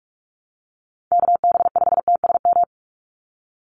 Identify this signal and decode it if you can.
Morse code, heard as a keyed tone, about 46 wpm, 690 Hz. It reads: KB5TSM
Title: USA Call Signs 45wpm